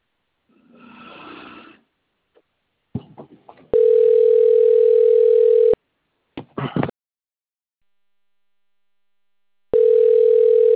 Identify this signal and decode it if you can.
Ringback tone (North America), 440 + 480 Hz.